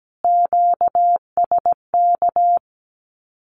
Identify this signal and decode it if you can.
Morse code, heard as a keyed tone, about 17 wpm, 700 Hz. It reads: QSK
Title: Q-Codes Repeated 3x 17wpm